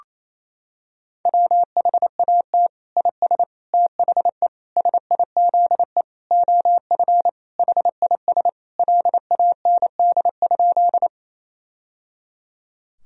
Morse code, heard as a keyed tone, about 28 wpm, 700 Hz. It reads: WHAT IS THE SIZE OF HIS LAND?
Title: Sentences from Top 1000 Words - Review Part 2 - 28wpm